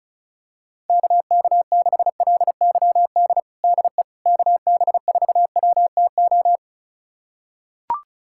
Morse code, heard as a keyed tone, about 34 wpm, 690 Hz. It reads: KK6LYD DE KB4WTO
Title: USA Callsign DE Callsign - 6 Characters - 35wpm